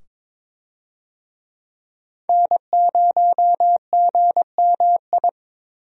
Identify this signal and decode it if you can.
Morse code, heard as a keyed tone, about 22 wpm, 700 Hz. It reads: N0GMI
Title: USA Call Signs - 5 Characters - 22wpm